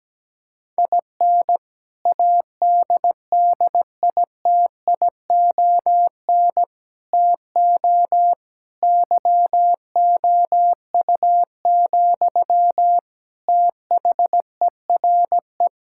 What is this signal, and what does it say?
Morse code, heard as a keyed tone, about 17 wpm, 700 Hz. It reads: IN ADDITION TO YOU, THERE
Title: Sentences from Top 1000 Words - Review Part 6 - 17wpm